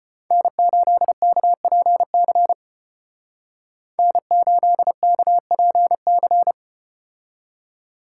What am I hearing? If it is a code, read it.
Morse code, heard as a keyed tone, about 36 wpm, 700 Hz. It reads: N8KPC N8KPC